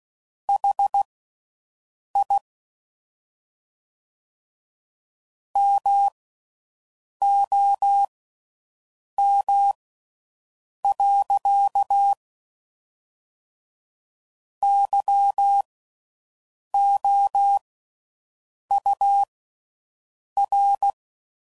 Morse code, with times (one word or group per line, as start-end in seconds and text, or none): 0.5-2.4 HI
5.6-12.1 MOM.
14.6-20.9 YOUR